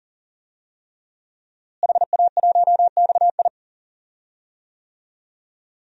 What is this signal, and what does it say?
Morse code, heard as a keyed tone, about 37 wpm, 700 Hz. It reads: HA1XI
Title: DXCC-100 Call Signs - 5 Characters - Repeated 3x - 40wpm